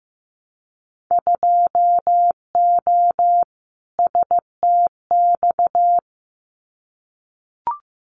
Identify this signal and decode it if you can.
Morse code, heard as a keyed tone, about 15 wpm, 700 Hz. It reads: 2O STX
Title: Winter Field Day Exchanges 15wpm